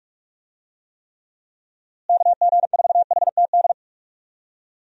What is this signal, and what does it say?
Morse code, heard as a keyed tone, about 45 wpm, 700 Hz. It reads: KG4HTD
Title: USA Call Signs - 6 Characters - Repeated 2x - 45wpm